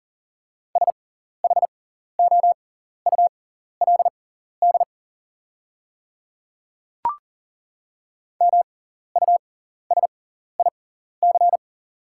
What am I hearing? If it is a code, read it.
Morse code, heard as a keyed tone, about 40 wpm, spaced as for 14 wpm, 690 Hz. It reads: SHOULD MUSIC
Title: 6x Character Spacing - Top 500 Words - 40wpm